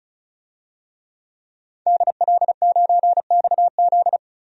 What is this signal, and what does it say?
Morse code, heard as a keyed tone, about 36 wpm, 700 Hz. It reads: DL9XZ
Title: DXCC-100 Call Signs - 5 Characters - 35wpm